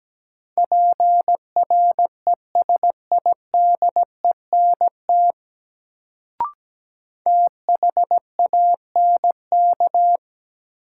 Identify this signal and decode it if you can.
Morse code, heard as a keyed tone, about 17 wpm, 700 Hz. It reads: PRESIDENT THANK
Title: Sets of 2 Words - New Word Drill - 17wpm